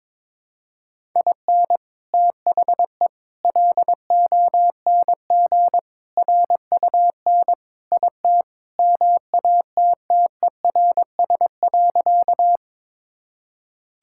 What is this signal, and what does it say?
Morse code, heard as a keyed tone, about 22 wpm, 700 Hz. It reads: IN THE LONG RUN IT MATTERS.
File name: practice / 22wpm / Sentences with 4-Word Phrases - 22wpm